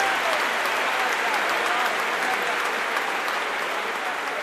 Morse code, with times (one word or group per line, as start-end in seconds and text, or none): none